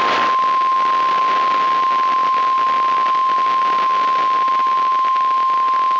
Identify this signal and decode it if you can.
Emergency Alert System (SAME) tones (the 1050 Hz attention tone), with noise. It is heard from 0.5 to 5.7 s.
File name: EAS WXM95 Towanda